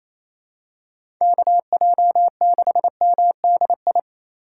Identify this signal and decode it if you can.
Morse code, heard as a keyed tone, about 28 wpm, 700 Hz. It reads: KJ6MDI
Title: USA Call Signs - 6 Characters - 2x Character Spacing - 28wpm